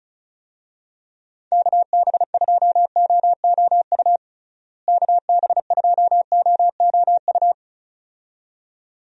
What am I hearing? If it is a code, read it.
Morse code, heard as a keyed tone, about 36 wpm, 700 Hz. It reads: KB2OOU KB2OOU
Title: USA Call Signs - 6 Characters - Repeated 2x - 35wpm